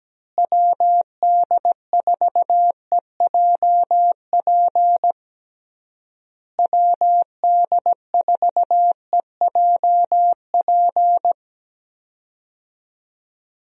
Morse code, heard as a keyed tone, about 17 wpm, 700 Hz. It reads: WD4EJP WD4EJP
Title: USA Call Signs - 6 Characters - Repeated 3x - 17wpm